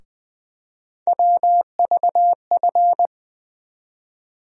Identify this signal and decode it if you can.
Morse code, heard as a keyed tone, about 20 wpm, 700 Hz. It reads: WVF
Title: USA Call Sign Postfixes - 3 Letters 20wpm